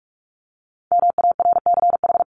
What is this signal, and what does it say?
Morse code, heard as a keyed tone, about 44 wpm, 700 Hz. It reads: MARCH
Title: Top 1501-1750 Words 45wpm